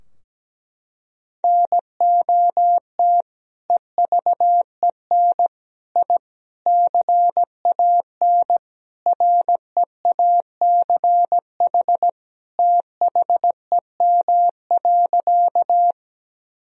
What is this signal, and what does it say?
Morse code, heard as a keyed tone, about 17 wpm, 700 Hz. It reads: NOT EVEN I CAN REACH THEM.
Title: Sentences from Top 1000 Words - Review Part 2 - 17wpm